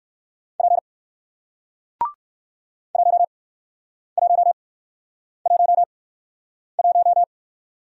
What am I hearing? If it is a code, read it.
Morse code, heard as a keyed tone, about 55 wpm, spaced as for 9 wpm, 700 Hz. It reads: P 1111
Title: Speed-Racing Single Letter-Number 45wpm